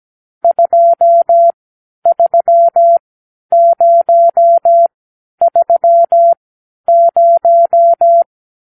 Morse code, heard as a keyed tone, about 17 wpm, spaced as for 10 wpm, 680 Hz. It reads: 23030
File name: arrl / Review (1-0)